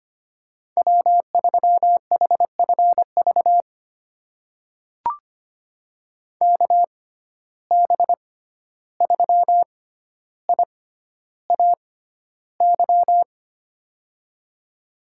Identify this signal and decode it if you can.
Morse code, heard as a keyed tone, about 25 wpm, spaced as for 12 wpm, 700 Hz. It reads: W3HFV KB3IAY
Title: USA Call Signs - 6x Character Spacing - 25wpm